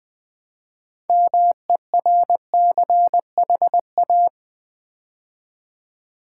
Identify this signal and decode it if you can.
Morse code, heard as a keyed tone, about 20 wpm, 700 Hz. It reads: MERCHA
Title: Build A Word - 3 Letters At A Time - 8-Letter Words - 20wpm